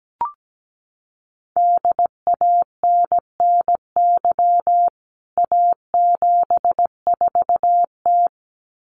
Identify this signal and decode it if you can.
Morse code, heard as a keyed tone, about 17 wpm, 700 Hz. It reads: DANNY A74T
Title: CWT Exchange 17wpm